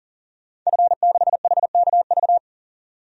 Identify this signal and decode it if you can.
Morse code, heard as a keyed tone, about 40 wpm, 700 Hz. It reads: F6HKV
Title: DXCC-100 Call Signs - 5 Characters - 40wpm